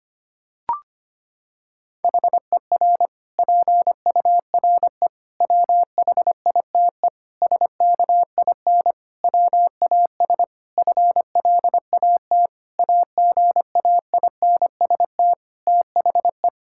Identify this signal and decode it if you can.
Morse code, heard as a keyed tone, about 25 wpm, 700 Hz. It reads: HER PURE WHITE SKIN WAS FLAT AGAINST THE
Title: Sentences from Top 1600 Words 25wpm